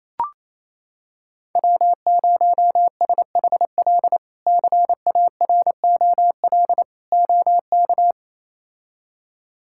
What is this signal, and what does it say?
Morse code, heard as a keyed tone, about 28 wpm, 700 Hz. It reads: W0SHL CAROL OK
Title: K1USN SST QSO Practice - 28wpm